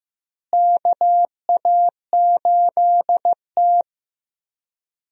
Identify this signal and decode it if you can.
Morse code, heard as a keyed tone, about 15 wpm, 700 Hz. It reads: KA8T
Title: USA Call Signs - 4 Characters - Repeated 3x - 15wpm